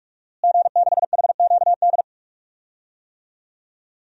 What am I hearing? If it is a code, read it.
Morse code, heard as a keyed tone, about 46 wpm, 700 Hz. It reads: G6HQD
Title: DXCC-100 Call Signs - 5 Characters - Repeated 3x - 45wpm